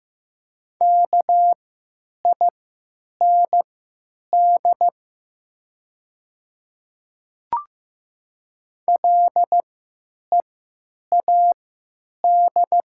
Morse code, heard as a keyed tone, about 15 wpm, spaced as for 9 wpm, 700 Hz. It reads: KIND LEAD
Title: 3x Character Spacing - Top 500 Words - 15wpm